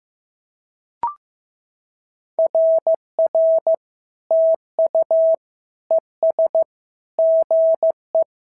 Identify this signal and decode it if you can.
Morse code, heard as a keyed tone, about 14 wpm, 650 Hz. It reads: RR TU ES GE
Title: SOTA QSO Practice 15wpm